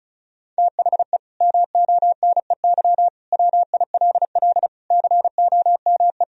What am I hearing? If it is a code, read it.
Morse code, heard as a keyed tone, about 36 wpm, 700 Hz. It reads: THE MONEY WILL COME
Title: Sentences from Top 1000 Words - Review Part 4 - 35wpm